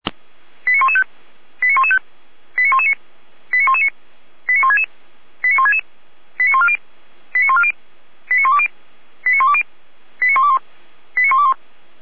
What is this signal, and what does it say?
Five-tone selective call (ZVEI), heard as quick runs of five tones, tones of about 70 ms. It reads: 89115 89115 89119 89119 89166 89166 89140 89140 89120 89120 89121 89121